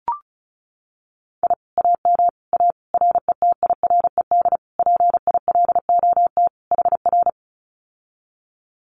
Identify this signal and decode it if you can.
Morse code, heard as a keyed tone, about 34 wpm, 700 Hz. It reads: I AM A RETIRED PILOT HR
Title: Common QSO Phrases Encore 35wpm